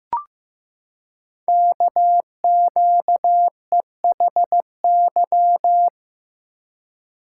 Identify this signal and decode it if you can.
Morse code, heard as a keyed tone, about 15 wpm, 700 Hz. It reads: KQEHY
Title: Code Groups - 5 Letters - 15wpm